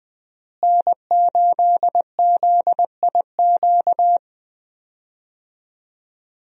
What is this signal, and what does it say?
Morse code, heard as a keyed tone, about 21 wpm, 700 Hz. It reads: N8ZIQ
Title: USA Call Signs - 5 Characters - Repeated 2x - 20wpm